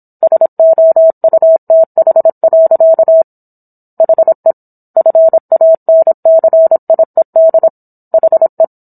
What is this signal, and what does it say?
Morse code, heard as a keyed tone, about 27 wpm, 650 Hz. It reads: SOUTH. HE FANCIED HE